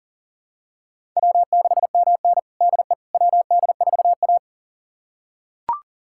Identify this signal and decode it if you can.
Morse code, heard as a keyed tone, about 40 wpm, 700 Hz. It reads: W6MN DE WD4A
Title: USA Callsign DE Callsign - 4 Characters - 40wpm